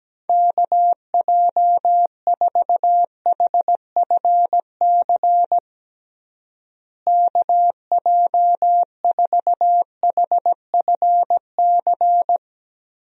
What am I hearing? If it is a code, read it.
Morse code, heard as a keyed tone, about 17 wpm, 700 Hz. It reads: KJ4HFC KJ4HFC
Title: USA Call Signs - 6 Characters - Repeated 3x - 17wpm